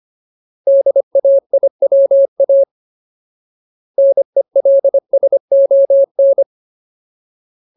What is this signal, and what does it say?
Morse code, heard as a keyed tone, about 25 wpm, 550 Hz. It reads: DAIWA NELSON
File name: Ultimate Word List / 25WPM Ultimate